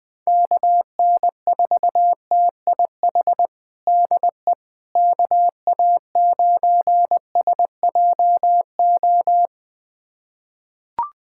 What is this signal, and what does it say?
Morse code, heard as a keyed tone, about 20 wpm, 700 Hz. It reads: KN4TIH DE KA9SJO
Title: USA Callsign DE Callsign - 6 Characters - 20wpm